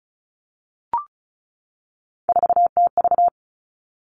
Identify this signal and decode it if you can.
Morse code, heard as a keyed tone, about 36 wpm, 700 Hz. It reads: 4TV
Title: Three Challenging Letter-Number Combinations 35wpm